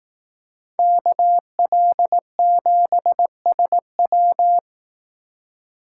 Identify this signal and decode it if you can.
Morse code, heard as a keyed tone, about 18 wpm, 700 Hz. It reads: KL7SW
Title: Speed Racing - USA 2x2 Call Signs 15wpm